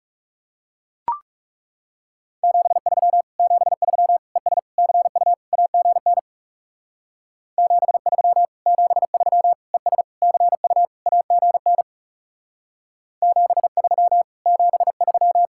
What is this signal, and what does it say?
Morse code, heard as a keyed tone, about 37 wpm, 700 Hz. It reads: 73 73 ES CU AGN 73 73 ES CU AGN 73 73
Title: Speed-Racing Common QSO Phrases - 30wpm